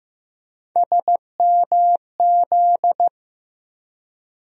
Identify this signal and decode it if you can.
Morse code, heard as a keyed tone, about 15 wpm, 700 Hz. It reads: SMZ
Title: USA Call Sign Postfixes - 3 Letters 15wpm